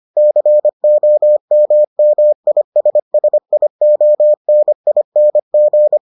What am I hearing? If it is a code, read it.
Morse code, heard as a keyed tone, about 25 wpm, 600 Hz. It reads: COMMISSIONING